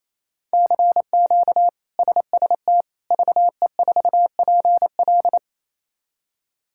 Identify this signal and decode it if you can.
Morse code, heard as a keyed tone, about 28 wpm, 700 Hz. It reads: CQ SST VE4PL
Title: K1USN SST QSO Practice - 28wpm